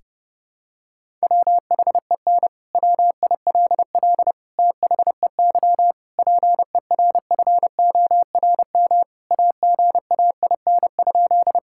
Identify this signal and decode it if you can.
Morse code, heard as a keyed tone, about 32 wpm, 700 Hz. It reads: WHEN WILL THEY PERFORM AGAIN?
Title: Sentences from Top 1000 Words - Review Part 5 - 30wpm